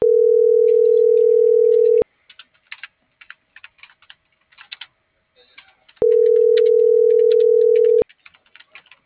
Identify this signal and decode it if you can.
Ringback tone (North America), 440 + 480 Hz.
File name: CENTRAL MAYAB / out lizbeth.mayab